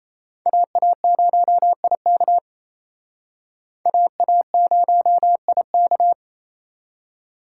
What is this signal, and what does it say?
Morse code, heard as a keyed tone, about 35 wpm, 700 Hz. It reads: AA0IK AA0IK
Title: Speed Racing - USA 2x2 Call Signs 28wpm